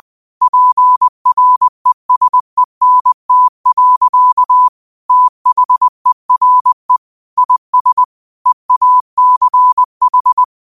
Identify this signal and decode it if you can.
Morse code, heard as a keyed tone, about 20 wpm, 1000 Hz. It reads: PRESENT. THERE IS EACH
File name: qotdfiles / 20wpm